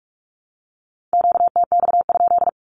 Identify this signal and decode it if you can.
Morse code, heard as a keyed tone, about 45 wpm, 700 Hz. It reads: QTX?
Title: Q-Codes Repeated 3x 45wpm